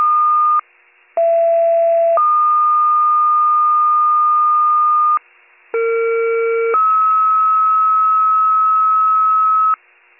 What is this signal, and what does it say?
Two-tone page, heard as a one-second tone, then a three-second tone, about 670 then 1220 Hz, about 460 then 1290 Hz. It is heard from 1.2 to 9.7 s.